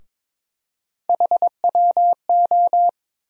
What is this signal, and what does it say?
Morse code, heard as a keyed tone, about 22 wpm, 700 Hz. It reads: HWO
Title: USA Call Sign Postfixes - 3 Letters 22wpm